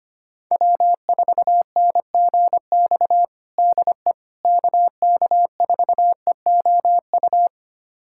Morse code, heard as a keyed tone, about 25 wpm, 700 Hz. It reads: W4NGX DE KK4EOU
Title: USA Callsign DE Callsign 25wpm